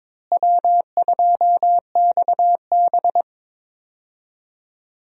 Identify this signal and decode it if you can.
Morse code, heard as a keyed tone, about 22 wpm, 700 Hz. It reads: W2XB
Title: USA Call Signs - 4 Characters - Repeated 3x - 22wpm